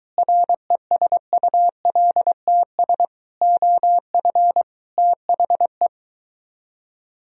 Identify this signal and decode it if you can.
Morse code, heard as a keyed tone, about 22 wpm, 700 Hz. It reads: RESULTS OF THE